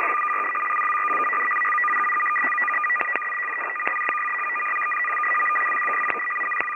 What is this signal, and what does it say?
Two-tone teleprinter signal, about 500 Hz shift.
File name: POL- -1410utc-fsk100-740